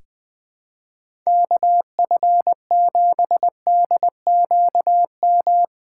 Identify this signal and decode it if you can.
Morse code, heard as a keyed tone, about 20 wpm, 700 Hz. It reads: KF7DQM
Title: USA Call Signs - 6 Characters - 8x Character Spacing - 20wpm